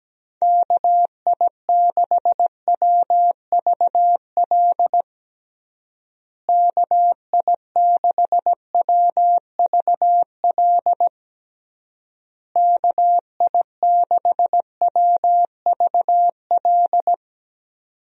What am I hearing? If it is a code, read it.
Morse code, heard as a keyed tone, about 17 wpm, 700 Hz. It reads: KI6WVL KI6WVL KI6WVL
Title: USA Call Signs - 6 Characters - Repeated 3x - 17wpm